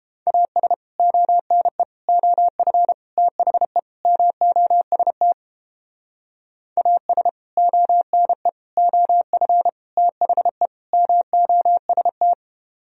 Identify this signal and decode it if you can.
Morse code, heard as a keyed tone, about 29 wpm, 700 Hz. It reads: AS ONE OF THE MOST AS ONE OF THE MOST